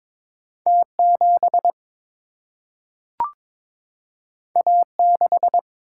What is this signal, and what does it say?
Morse code, heard as a keyed tone, about 22 wpm, 700 Hz. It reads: T7 A6
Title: Single-Letter Single-Number 22wpm